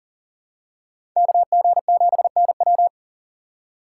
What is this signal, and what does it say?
Morse code, heard as a keyed tone, about 40 wpm, 700 Hz. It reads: KG7NW
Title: USA Call Signs - 3x Character Spacing - 40wpm